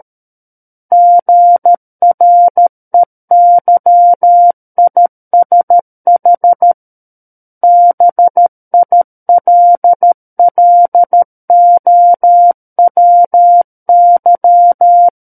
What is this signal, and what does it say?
Morse code, heard as a keyed tone, about 13 wpm, 700 Hz. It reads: GREYISH BILLOWY
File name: Book One / 04-Chapter 4 - 13 WPM